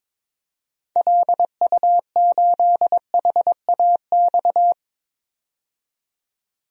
Morse code, heard as a keyed tone, about 21 wpm, 700 Hz. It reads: LU8HAX
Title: DXCC-100 Call Signs - 6 Characters - 22wpm